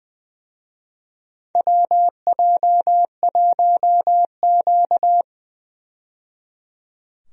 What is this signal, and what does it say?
Morse code, heard as a keyed tone, about 20 wpm, 700 Hz. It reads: WJ1Q
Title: USA Call Signs - 4 Characters - 20wpm